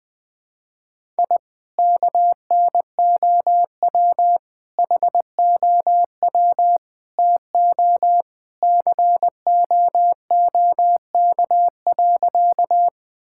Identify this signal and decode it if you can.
Morse code, heard as a keyed tone, about 20 wpm, 700 Hz. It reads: I KNOW HOW TO COOK.
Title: Sentences with 3-Word Phrases - 20wpm